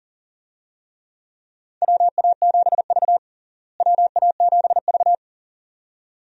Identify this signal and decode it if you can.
Morse code, heard as a keyed tone, about 40 wpm, 700 Hz. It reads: WA7V WA7V
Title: USA Call Signs - 4 Characters - Repeated 2x - 40wpm